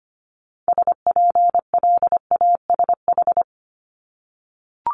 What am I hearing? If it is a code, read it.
Morse code, heard as a keyed tone, about 25 wpm, 700 Hz. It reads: SPLASH
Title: Multi-letter Phonemes 25wpm